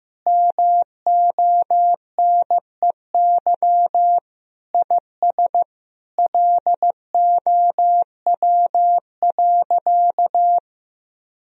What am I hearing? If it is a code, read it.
Morse code, heard as a keyed tone, about 16 wpm, 700 Hz. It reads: MONEY IS LOW.
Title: Sentences from Top 1000 Words - Review Part 1 - 15wpm